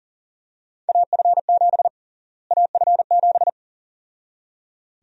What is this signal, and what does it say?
Morse code, heard as a keyed tone, about 40 wpm, 700 Hz. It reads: AF7 AF7
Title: USA Call Sign Prefixes - Repeated 2x - 40wpm